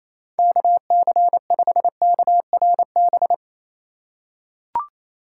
Morse code, heard as a keyed tone, about 28 wpm, 700 Hz. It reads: KC5KRB
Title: USA Call Signs - 6 Characters - 8x Character Spacing - 28wpm